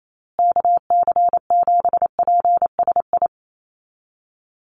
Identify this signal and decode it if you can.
Morse code, heard as a keyed tone, about 28 wpm, 700 Hz. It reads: KC7PSI
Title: USA Call Signs Repeated 3x 28wpm